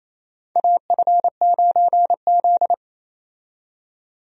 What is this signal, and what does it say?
Morse code, heard as a keyed tone, about 28 wpm, 700 Hz. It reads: AF9Z
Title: USA Call Signs - 4 Characters - 28wpm